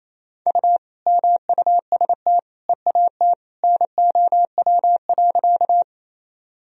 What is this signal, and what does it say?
Morse code, heard as a keyed tone, about 28 wpm, 700 Hz. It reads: U MUST EAT NOW.
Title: Sentences from Top 300 Words and Abbreviations 28wpm